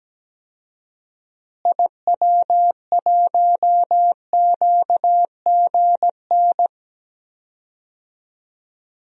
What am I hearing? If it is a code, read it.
Morse code, heard as a keyed tone, about 17 wpm, 700 Hz. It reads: IW1QGN
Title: DXCC-100 Call Signs - 6 Characters - Repeated 2x - 17wpm